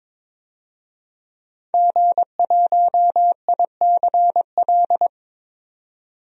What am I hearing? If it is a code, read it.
Morse code, heard as a keyed tone, about 22 wpm, 700 Hz. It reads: G1ICL
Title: DXCC-100 Call Signs - 5 Characters - 22wpm